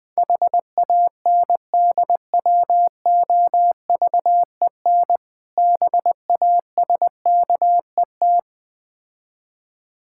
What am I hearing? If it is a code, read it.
Morse code, heard as a keyed tone, about 20 wpm, 700 Hz. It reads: HANDWOVEN BASKET
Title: Difficult Words with Hints - 20wpm